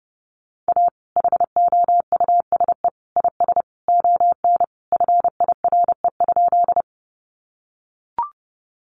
Morse code, heard as a keyed tone, about 29 wpm, 700 Hz. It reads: A HOUSE IS ON FIRE?
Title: Sentences from Top 400 Words 30wpm